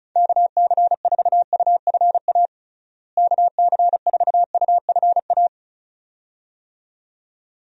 Morse code, heard as a keyed tone, about 34 wpm, 700 Hz. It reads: KC4UFA KC4UFA
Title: USA Call Signs - 6 Characters - Repeated 2x - 35wpm